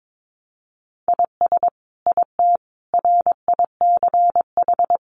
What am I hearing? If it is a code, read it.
Morse code, heard as a keyed tone, about 22 wpm, 700 Hz. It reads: IS IT RICH
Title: Sentences from Top 1100 Words 22wpm